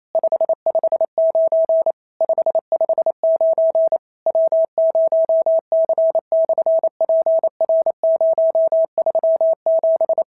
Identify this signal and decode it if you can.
Morse code, heard as a keyed tone, about 28 wpm, 650 Hz. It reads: 559 559 W0C/PR037
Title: SOTA QSO Practice 28wpm